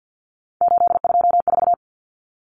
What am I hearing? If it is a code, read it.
Morse code, heard as a keyed tone, about 48 wpm, 700 Hz. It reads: 824
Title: Common Cut Numbers - 3-Digit 50wpm